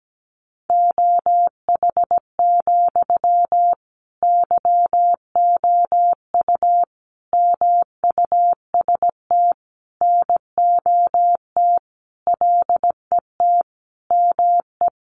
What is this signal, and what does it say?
Morse code, heard as a keyed tone, about 17 wpm, 700 Hz. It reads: OH, YOU MUST NOT LET ME
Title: Sentences from Top 1100 Words 17wpm